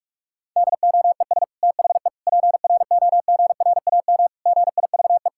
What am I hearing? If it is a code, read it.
Morse code, heard as a keyed tone, about 46 wpm, 700 Hz. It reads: DOES THE PROGRAM GIVE
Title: Sentences from Top 1000 Words - Review Part 5 - 45wpm